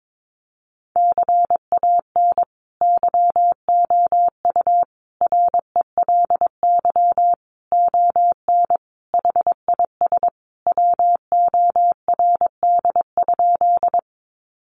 Morse code, heard as a keyed tone, about 22 wpm, 700 Hz. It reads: CAN YOU RELY ON HIS WORD?
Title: Sentences from Top 1200 Words 22wpm